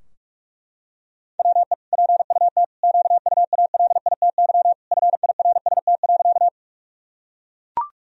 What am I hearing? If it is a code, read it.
Morse code, heard as a keyed tone, about 43 wpm, 700 Hz. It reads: WE PUT QUALITY FIRST.
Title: Sentences from Top 1000 Words - Review Part 4 - 45wpm